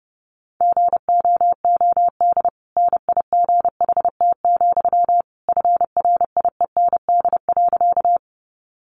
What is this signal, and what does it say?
Morse code, heard as a keyed tone, about 30 wpm, 700 Hz. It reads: GOOD NIGHT, FRIEND.
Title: Sentences from Top 300 Words 30wpm